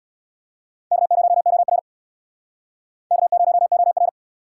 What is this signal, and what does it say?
Morse code, heard as a keyed tone, about 68 wpm, 700 Hz. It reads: K0OR K0OR